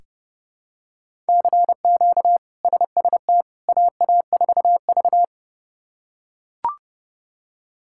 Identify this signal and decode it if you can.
Morse code, heard as a keyed tone, about 30 wpm, 700 Hz. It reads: CQ SST AA4V
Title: K1USN SST QSO Practice - 30wpm